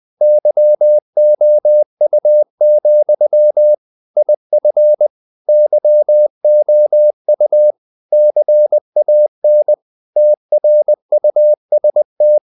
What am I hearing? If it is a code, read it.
Morse code, heard as a keyed tone, about 20 wpm, 600 Hz. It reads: YOU, IF YOU CAN TRUST
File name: CW / Practice / If20